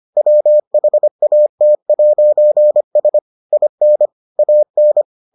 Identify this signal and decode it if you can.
Morse code, heard as a keyed tone, about 25 wpm, 600 Hz. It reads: WHAT'S IN AN